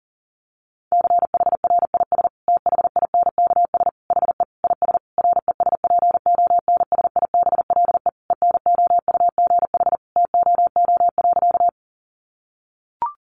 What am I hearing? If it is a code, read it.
Morse code, heard as a keyed tone, about 37 wpm, 700 Hz. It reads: CHRIS THINKS HE IS RESPONSIBLE ENOUGH TOO.
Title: Additional Easy Phrases 40wpm